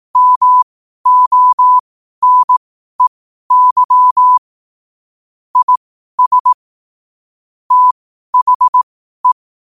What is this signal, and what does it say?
Morse code, heard as a keyed tone, about 18 wpm, spaced as for 12 wpm, 1000 Hz. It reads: MONEY IS THE
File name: qotdfiles / 12wpm